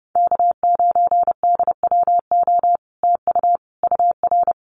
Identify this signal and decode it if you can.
Morse code, heard as a keyed tone, about 31 wpm, 700 Hz. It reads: K9DWO TU UR
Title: Common QSO Phrases Encore 30wpm